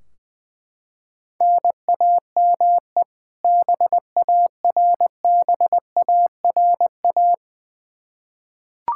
Morse code, heard as a keyed tone, about 21 wpm, 700 Hz. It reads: NAME BARBARA
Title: Practice QSO - Exchange 1 - Beginning - 20wpm